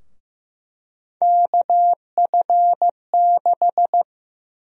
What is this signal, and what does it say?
Morse code, heard as a keyed tone, about 15 wpm, 700 Hz. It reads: KF6
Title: USA Call Sign Prefixes - Repeated 2x - 15wpm